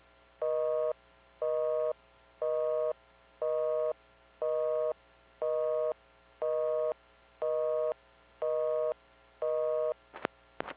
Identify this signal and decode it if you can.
Busy tone (North America), 480 + 620 Hz, with noise.